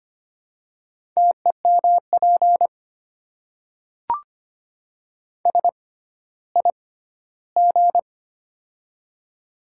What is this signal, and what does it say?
Morse code, heard as a keyed tone, about 25 wpm, spaced as for 14 wpm, 700 Hz. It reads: TEMP SIG
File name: Top 90 QSO Elements - 6x Character Spacing - 25wpm